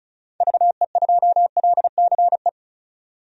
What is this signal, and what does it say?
Morse code, heard as a keyed tone, about 36 wpm, 700 Hz. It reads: VE2LCE
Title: DXCC-100 Call Signs - 6 Characters - 35wpm